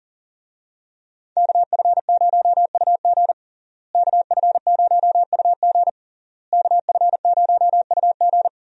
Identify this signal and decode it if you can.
Morse code, heard as a keyed tone, about 40 wpm, 690 Hz. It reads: KF0UG KF0UG KF0UG
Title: USA Call Signs - 5 Characters - Repeated 3x - 40wpm